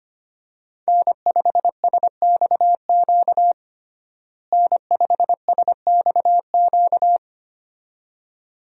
Morse code, heard as a keyed tone, about 25 wpm, 700 Hz. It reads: N5SXQ N5SXQ
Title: USA Call Signs - 5 Characters - Repeated 2x - 25wpm